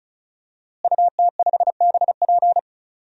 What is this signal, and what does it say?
Morse code, heard as a keyed tone, about 34 wpm, 700 Hz. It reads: UT5BP
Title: DXCC-100 Call Signs - 5 Characters - Repeated 3x - 35wpm